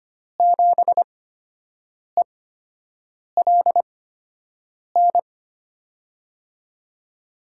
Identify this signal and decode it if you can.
Morse code, heard as a keyed tone, about 25 wpm, spaced as for 7 wpm, 700 Hz. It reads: 7ELN
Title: USA Call Signs - 5 Characters - 8x Character Spacing - 25wpm